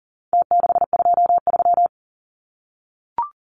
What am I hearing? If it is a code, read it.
Morse code, heard as a keyed tone, about 40 wpm, 700 Hz. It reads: T623
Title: Numbers - Contest Style - 4-Digit 40wpm